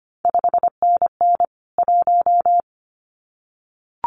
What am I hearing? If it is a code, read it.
Morse code, heard as a keyed tone, about 25 wpm, 700 Hz. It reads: 5NN 1
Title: CQ WW DX Contest Exchange Encore - 25wpm